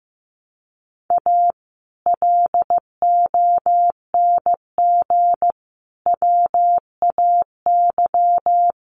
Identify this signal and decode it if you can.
Morse code, heard as a keyed tone, about 15 wpm, 700 Hz. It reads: A LONG WAY
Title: Sets of 5 Words 15wpm